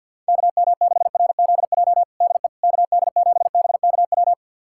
Morse code, heard as a keyed tone, about 51 wpm, 700 Hz. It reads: KM6RZJ DE KD7BKW
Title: USA Callsign DE Callsign - 6 Characters - 50wpm